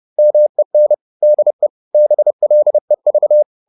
Morse code, heard as a keyed tone, about 30 wpm, 600 Hz. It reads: MEN DE BLEV
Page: Ferie 30 wpm